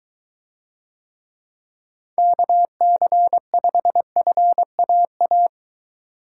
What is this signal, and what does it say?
Morse code, heard as a keyed tone, about 23 wpm, 700 Hz. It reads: KC5FAA